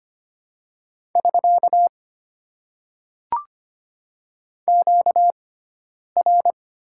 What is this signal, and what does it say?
Morse code, heard as a keyed tone, about 25 wpm, 700 Hz. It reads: <SK> Q R
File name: Top 90 QSO Elements - 6x Character Spacing - 25wpm